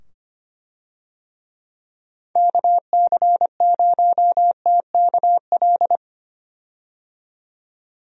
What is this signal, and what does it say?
Morse code, heard as a keyed tone, about 25 wpm, 700 Hz. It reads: KC0TKL
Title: USA Call Signs - 6 Characters - 25wpm